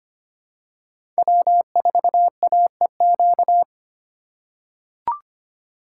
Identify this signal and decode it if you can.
Morse code, heard as a keyed tone, about 25 wpm, 700 Hz. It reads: W4AEQ
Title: USA Call Signs - 5 Characters - 4x Character Spacing - 25wpm